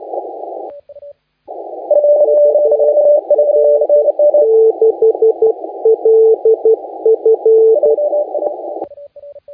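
Morse code, heard as a keyed tone, about 12 wpm, spaced as for 10 wpm, 440 Hz. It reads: UA6LF
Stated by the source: Morse contest audio recorded by LA4C during the SAC CW 2018 contest.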